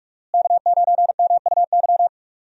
Morse code, heard as a keyed tone, about 46 wpm, 700 Hz. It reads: K9MUY
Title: USA Call Signs - 5 Characters - Repeated 3x - 45wpm